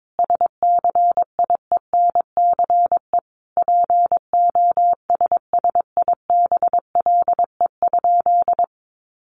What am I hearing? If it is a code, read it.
Morse code, heard as a keyed tone, about 22 wpm, 700 Hz. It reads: SCIENCE POSSIBLE?
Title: Sentences from Top 1500 Words 22wpm